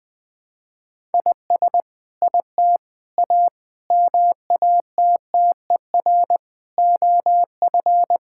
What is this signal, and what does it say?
Morse code, heard as a keyed tone, about 20 wpm, 700 Hz. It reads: IS IT A MATTER OF
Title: Sentences from Top 1000 Words - Review Part 5 - 20wpm